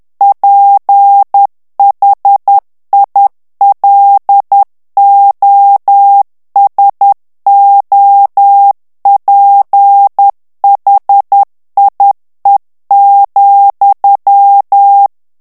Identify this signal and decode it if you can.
Morse code, heard as a keyed tone, about 10 wpm, 800 Hz. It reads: PHILOSOPHIE,
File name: CW / FICHIERS-MP3 / FRANCAIS TEXTE CLAIR 2 VITESSE 10 MOTS MINUTES